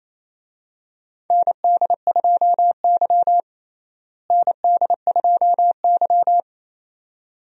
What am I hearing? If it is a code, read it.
Morse code, heard as a keyed tone, about 28 wpm, 700 Hz. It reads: ND2Y ND2Y
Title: USA Call Signs - 4 Characters - Repeated 2x - 28wpm